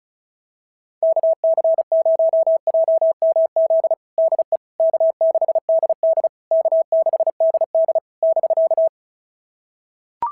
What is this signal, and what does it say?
Morse code, heard as a keyed tone, about 36 wpm, 660 Hz. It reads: KC0JMZ DE K6DD K6DD <BK>
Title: SOTA QSO Practice 35wpm